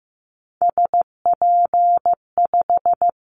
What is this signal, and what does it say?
Morse code, heard as a keyed tone, about 15 wpm, 700 Hz. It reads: SP5
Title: Callsign Prefixes for Top 25 DXCC - 15wpm